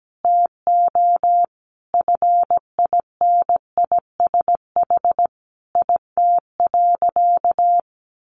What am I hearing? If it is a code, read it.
Morse code, heard as a keyed tone, about 17 wpm, 700 Hz. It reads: TO FINISH IT.
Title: Additional Easy Phrases 17wpm